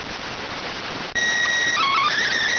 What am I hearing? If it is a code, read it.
SSTV (slow-scan television), PD 290.